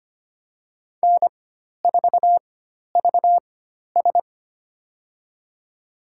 Morse code, heard as a keyed tone, about 25 wpm, spaced as for 12 wpm, 700 Hz. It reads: N4VS
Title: USA Call Signs - 4 Characters - 4x Character Spacing - 25wpm